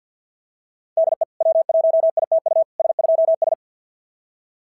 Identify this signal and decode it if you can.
Morse code, heard as a keyed tone, about 47 wpm, 690 Hz. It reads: DE W1ITU S2S
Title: SOTA QSO Practice 50wpm